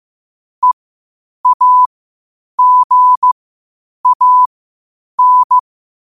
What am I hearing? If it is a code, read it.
Morse code, heard as a keyed tone, about 15 wpm, spaced as for 8 wpm, 1000 Hz. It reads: EAGAN
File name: qotdfiles / 8wpm